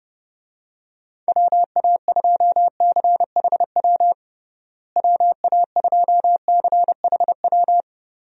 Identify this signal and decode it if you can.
Morse code, heard as a keyed tone, about 30 wpm, 700 Hz. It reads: WA2CHW WA2CHW
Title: USA Call Signs - 6 Characters - Repeated 3x - 30wpm